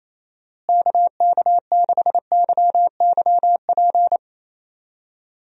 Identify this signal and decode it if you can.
Morse code, heard as a keyed tone, about 28 wpm, 700 Hz. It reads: KK6YYP
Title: USA Call Signs - 6 Characters - Repeated 2x - 28wpm